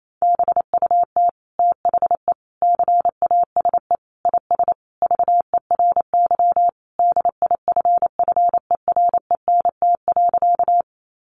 Morse code, heard as a keyed tone, about 28 wpm, 700 Hz. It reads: BUT THE CASE IS VERY DIFFERENT.
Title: Sentences from Top 700 Words 28wpm